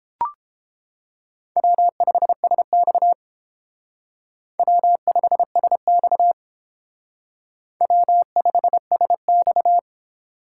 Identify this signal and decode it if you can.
Morse code, heard as a keyed tone, about 31 wpm, 700 Hz. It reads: W5SX W5SX W5SX